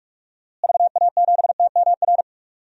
Morse code, heard as a keyed tone, about 45 wpm, 700 Hz. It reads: VA7TMR
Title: DXCC-100 Call Signs - 6 Characters - Repeated 3x - 45wpm